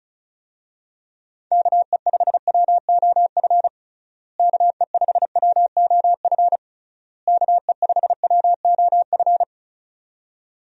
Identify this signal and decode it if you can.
Morse code, heard as a keyed tone, about 34 wpm, 700 Hz. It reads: KE5WOF KE5WOF KE5WOF
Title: USA Call Signs - 6 Characters - Repeated 3x - 35wpm